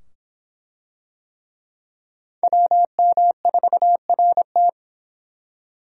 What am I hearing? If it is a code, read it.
Morse code, heard as a keyed tone, about 24 wpm, 700 Hz. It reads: WM4RT